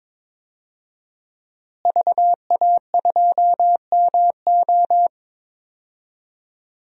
Morse code, heard as a keyed tone, about 21 wpm, 700 Hz. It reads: VA2MO
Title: DXCC-100 Call Signs - 5 Characters - Repeated 3x - 22wpm